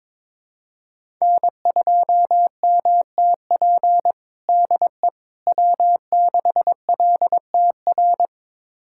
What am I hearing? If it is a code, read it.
Morse code, heard as a keyed tone, about 21 wpm, 700 Hz. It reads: N2MTP DE W6LTR
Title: USA Callsign DE Callsign - 5 Characters - 22wpm